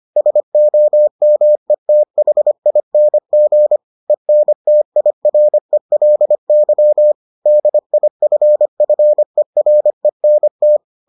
Morse code, heard as a keyed tone, about 25 wpm, 600 Hz. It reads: SOMETHING ENTIRELY DIFFERENT